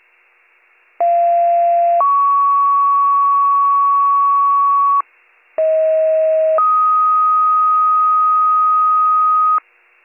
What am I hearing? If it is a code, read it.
Two-tone page, heard as a one-second tone, then a three-second tone, about 690 then 1120 Hz, about 640 then 1250 Hz. It is heard from 1.0 to 9.6 s.